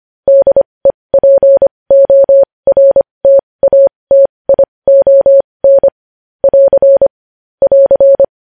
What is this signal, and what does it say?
Morse code, heard as a keyed tone, about 25 wpm, spaced as for 20 wpm, 550 Hz. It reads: DEPORTATION <AR> <AR>
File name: podcast / News headlines-1x-25x20